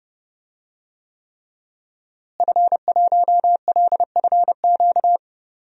Morse code, heard as a keyed tone, about 30 wpm, 700 Hz. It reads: F1LFQ
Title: DXCC-100 Call Signs - 5 Characters - Repeated 3x - 30wpm